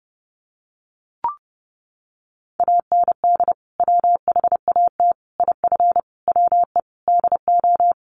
Morse code, heard as a keyed tone, about 30 wpm, 700 Hz. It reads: AND WHAT IF WE DO
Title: Sentences from Top 1500 Words 30wpm